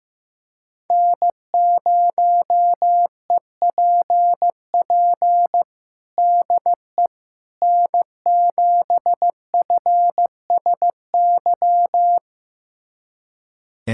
Morse code, heard as a keyed tone, about 15 wpm, 700 Hz. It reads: N0EPP DE N7FSY
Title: USA Callsign DE Callsign - 5 Characters - 15wpm